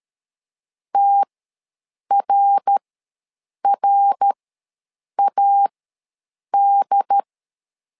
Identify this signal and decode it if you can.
Morse code, heard as a keyed tone, about 13 wpm, spaced as for 7 wpm, 800 Hz. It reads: TRRAD
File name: LEZIONI / 3 MWRO / 3 MWR ws 12wpm 1809